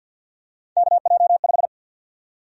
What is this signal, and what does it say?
Morse code, heard as a keyed tone, about 46 wpm, 700 Hz. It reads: KJ5
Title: USA Call Sign Prefixes - 50wpm